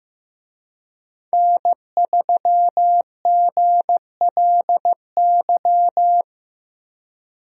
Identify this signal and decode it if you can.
Morse code, heard as a keyed tone, about 15 wpm, 700 Hz. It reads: N3GLY
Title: USA Call Signs - 5 Characters - Repeated 3x - 15wpm